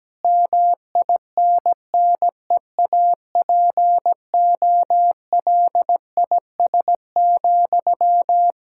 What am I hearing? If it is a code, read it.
Morse code, heard as a keyed tone, about 17 wpm, 700 Hz. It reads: MINNEAPOLIS,
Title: Practice QSO - Exchange 1 - Beginning - 17wpm